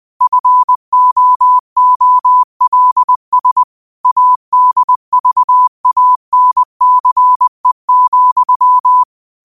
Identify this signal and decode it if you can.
Morse code, heard as a keyed tone, about 20 wpm, 1000 Hz. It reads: FOOLS ADVANCE,
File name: qotdfiles / 20wpm